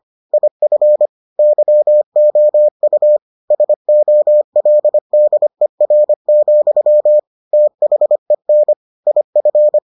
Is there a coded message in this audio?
Yes: Morse code from 0.3 to 9.8 s, heard as a keyed tone, about 25 wpm, 600 Hz. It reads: IF YOU SOLDER, THEN IF